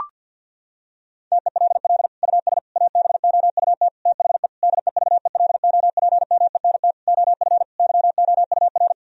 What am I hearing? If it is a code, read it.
Morse code, heard as a keyed tone, about 50 wpm, 700 Hz. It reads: TELL US ABOUT THE DEVELOPMENT OF YOUR
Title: Sentences from Top 1000 Words - Review Part 4 - 50wpm